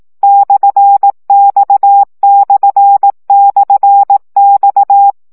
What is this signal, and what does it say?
Morse code, heard as a keyed tone, about 18 wpm, 800 Hz. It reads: /X//X
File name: CW gyakorlas x per 15 baud